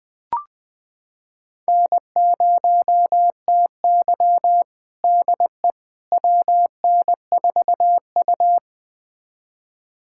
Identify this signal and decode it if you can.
Morse code, heard as a keyed tone, about 20 wpm, 700 Hz. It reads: N0TY DE WN4U
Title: USA Callsign DE Callsign - 4 Characters - 20wpm